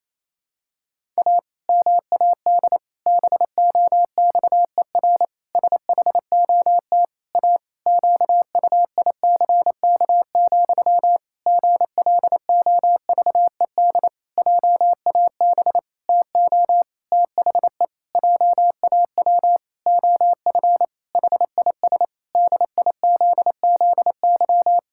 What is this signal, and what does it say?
Morse code, heard as a keyed tone, about 28 wpm, 700 Hz. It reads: A MAD BOXER SHOT A QUICK, GLOVED JAB TO THE JAW OF HIS DIZZY
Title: Pangram Challenge v2 - 28wpm